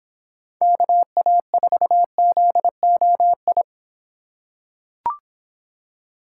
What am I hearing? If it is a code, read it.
Morse code, heard as a keyed tone, about 26 wpm, 700 Hz. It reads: KA4ZOI